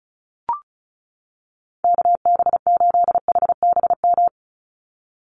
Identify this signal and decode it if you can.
Morse code, heard as a keyed tone, about 34 wpm, 700 Hz. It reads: KB8HBM
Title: USA Call Signs 35wpm